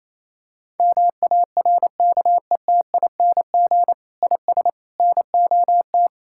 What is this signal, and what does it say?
Morse code, heard as a keyed tone, about 27 wpm, 700 Hz. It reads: MARKETING IS NOT
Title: Sentences from Top 1000 Words - Review Part 5 - 28wpm